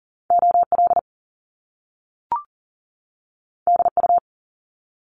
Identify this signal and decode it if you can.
Morse code, heard as a keyed tone, about 39 wpm, 700 Hz. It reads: OL DU
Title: Top Two Letter Combinations in Words 40wpm